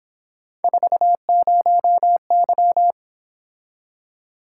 Morse code, heard as a keyed tone, about 27 wpm, 700 Hz. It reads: 40Y
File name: Speed-Racing - Three Letter-Number Combinations - 22wpm